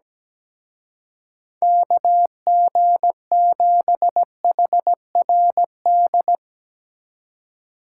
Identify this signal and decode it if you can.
Morse code, heard as a keyed tone, about 17 wpm, 700 Hz. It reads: KG7HRD
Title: Speed Racing - USA Call Signs 17wpm